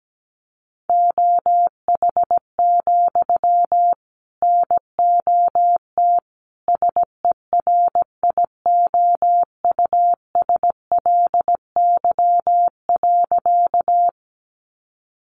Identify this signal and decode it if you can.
Morse code, heard as a keyed tone, about 17 wpm, 700 Hz. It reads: OH, NOT SERIOUSLY.
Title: Sentences from Top 1200 Words 17wpm